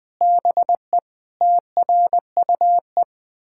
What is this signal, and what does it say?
Morse code, heard as a keyed tone, about 20 wpm, 700 Hz. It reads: BE TRUE
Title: Sentences from Top 1000 Words - Review Part 6 - 20wpm